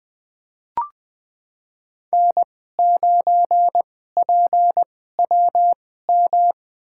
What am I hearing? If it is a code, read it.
Morse code, heard as a keyed tone, about 20 wpm, spaced as for 14 wpm, 700 Hz. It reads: N9PWM
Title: USA Call Signs - 2x Character Spacing - 20wpm